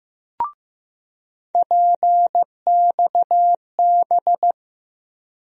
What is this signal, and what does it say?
Morse code, heard as a keyed tone, about 15 wpm, 700 Hz. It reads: PXB
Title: USA Call Sign Postfixes - 3 Letters 15wpm